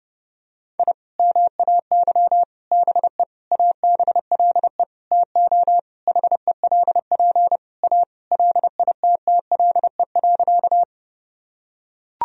Morse code, heard as a keyed tone, about 30 wpm, 700 Hz. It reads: I MAY BE ABLE TO HELP A LITTLE.
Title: Sentences from Top 1000 Words - Review Part 2 - 30wpm